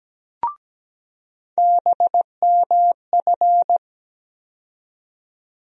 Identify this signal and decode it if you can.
Morse code, heard as a keyed tone, about 17 wpm, 700 Hz. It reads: BMF
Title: USA Call Sign Postfixes - 3 Letters 17wpm